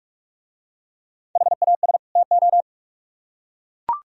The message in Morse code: HAS TO

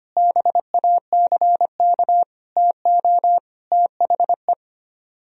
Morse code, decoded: BACK TO THE

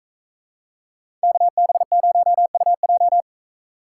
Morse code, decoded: KB0UJ

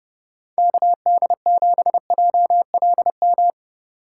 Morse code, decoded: KD7JLM

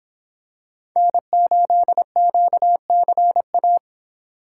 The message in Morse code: N8QCA

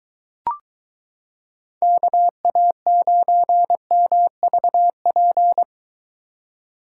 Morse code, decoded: KA9MVP

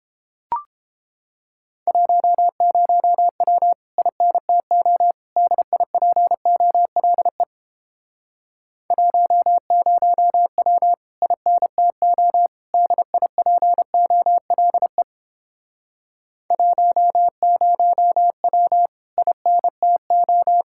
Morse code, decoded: 10W INTO DIPOLE 10W INTO DIPOLE 10W INTO

